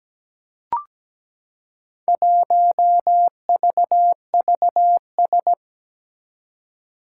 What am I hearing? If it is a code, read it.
Morse code, heard as a keyed tone, about 17 wpm, 700 Hz. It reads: 1VVS
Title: Four Challenging Letter-Number Combinations 17wpm